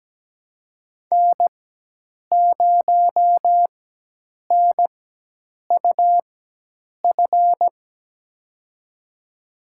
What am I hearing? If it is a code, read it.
Morse code, heard as a keyed tone, about 17 wpm, spaced as for 8 wpm, 700 Hz. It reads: N0NUF
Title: USA Call Signs - 4x Character Spacing - 17wpm